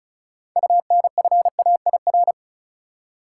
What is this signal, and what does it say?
Morse code, heard as a keyed tone, about 34 wpm, 700 Hz. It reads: UNFAIR
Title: Six Letter Words - 35wpm